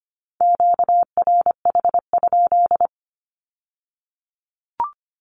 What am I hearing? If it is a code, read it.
Morse code, heard as a keyed tone, about 25 wpm, 700 Hz. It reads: QRH?
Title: Q-Codes with Spoken Meaning - 25wpm